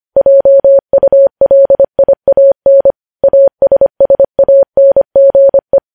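Morse code, heard as a keyed tone, about 25 wpm, 550 Hz. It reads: JULIAN ASSANGE
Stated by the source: Morse practice sending news headlines at 25 wpm.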